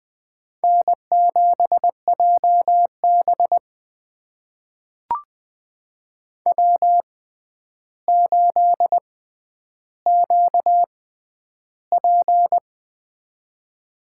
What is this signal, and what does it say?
Morse code, heard as a keyed tone, about 20 wpm, spaced as for 10 wpm, 700 Hz. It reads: N7JB W8QP
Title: USA Call Signs - 4 Characters - 6x Character Spacing - 20wpm